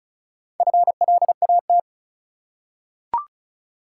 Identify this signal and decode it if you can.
Morse code, heard as a keyed tone, about 36 wpm, 700 Hz. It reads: FLAT
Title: Four Letter Words - 35wpm